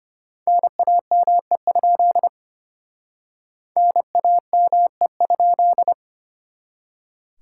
Morse code, heard as a keyed tone, about 31 wpm, 700 Hz. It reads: NAME? NAME?
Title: Speed-Racing Common QSO Phrases Encore - 25wpm